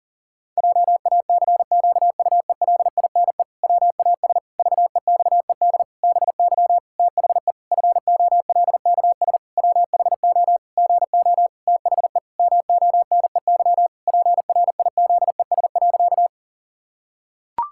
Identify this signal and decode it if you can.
Morse code, heard as a keyed tone, about 40 wpm, 700 Hz. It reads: JACQUELINE WAS VEXED BY THE FOLKS WHO GO THE MONEY PRIZES.
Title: Pangram Challenge v2 - 40wpm